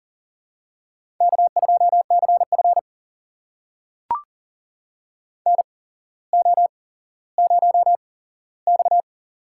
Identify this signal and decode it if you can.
Morse code, heard as a keyed tone, about 40 wpm, spaced as for 17 wpm, 700 Hz. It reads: K2CF NO0X
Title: USA Call Signs - 8x Character Spacing - 40wpm